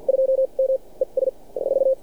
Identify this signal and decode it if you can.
Morse code, heard as a keyed tone, about 47 wpm, spaced as for 36 wpm, 530 Hz. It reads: 2M ES 4
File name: archived / CRSnet